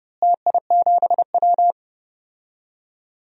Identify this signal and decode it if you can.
Morse code, heard as a keyed tone, about 32 wpm, 700 Hz. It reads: TI7W
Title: DXCC-100 Call Signs - 4 Characters - Repeated 2x - 30wpm